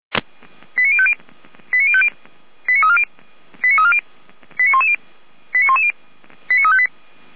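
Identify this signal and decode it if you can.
Five-tone selective call (ZVEI), heard as quick runs of five tones, tones of about 70 ms. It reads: 89050 89050 89340 89340 89110 89110 89358
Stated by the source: ZVEI five-tone call received by SDR.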